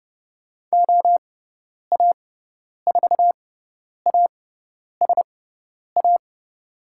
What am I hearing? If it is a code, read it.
Morse code, heard as a keyed tone, about 30 wpm, spaced as for 10 wpm, 700 Hz. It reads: OA4ASA
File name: Lesson 17g - Letter u - Call Signs 30wpm